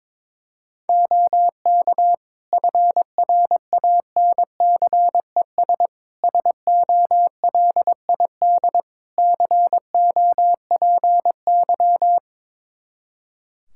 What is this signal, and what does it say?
Morse code, heard as a keyed tone, about 22 wpm, 700 Hz. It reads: OK FRANCES SOLID COPY
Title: Practice QSO - Exchange 3 - More Info - 22wpm